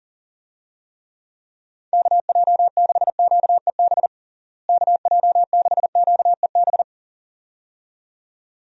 Morse code, heard as a keyed tone, about 37 wpm, 700 Hz. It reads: KJ6QEB KJ6QEB
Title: USA Call Signs - 6 Characters - Repeated 2x - 40wpm